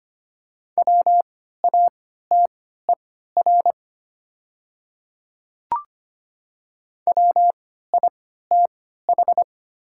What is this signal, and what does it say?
Morse code, heard as a keyed tone, about 25 wpm, spaced as for 14 wpm, 700 Hz. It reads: WATER WITH
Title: 3x Character Spacing - Top 100 Words - 25wpm